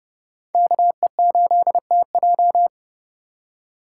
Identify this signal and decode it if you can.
Morse code, heard as a keyed tone, about 30 wpm, 700 Hz. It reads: KE8TJ
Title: USA Call Signs - 5 Characters - 4x Character Spacing - 30wpm